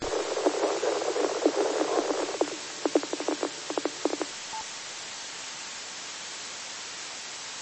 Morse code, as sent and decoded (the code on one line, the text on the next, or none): none